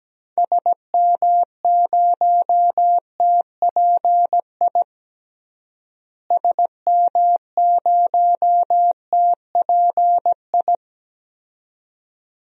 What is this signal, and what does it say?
Morse code, heard as a keyed tone, about 17 wpm, 700 Hz. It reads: SM0TPI SM0TPI
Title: DXCC-100 Call Signs - 6 Characters - Repeated 2x - 17wpm